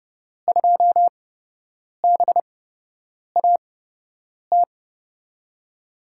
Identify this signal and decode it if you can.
Morse code, heard as a keyed tone, about 30 wpm, spaced as for 8 wpm, 700 Hz. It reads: 2BAT
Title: USA Call Signs - 6 Characters - 8x Character Spacing - 30wpm